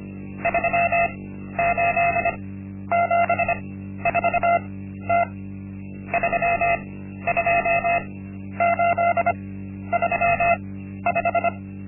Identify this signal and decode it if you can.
Morse code, heard as a keyed tone, about 25 wpm, spaced as for 12 wpm, 690 Hz. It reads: 3874T 32835
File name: Linksmakalnis-1